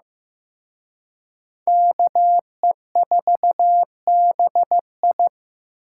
Morse code, heard as a keyed tone, about 18 wpm, 700 Hz. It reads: KE4BI
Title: Speed Racing - USA Call Signs 15wpm